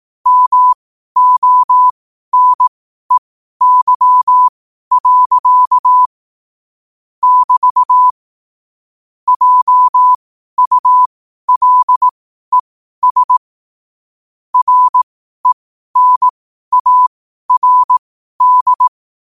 Morse code, heard as a keyed tone, about 18 wpm, spaced as for 12 wpm, 1000 Hz. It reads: MONEY. <BT> JULES RENARD
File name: qotdfiles / 12wpm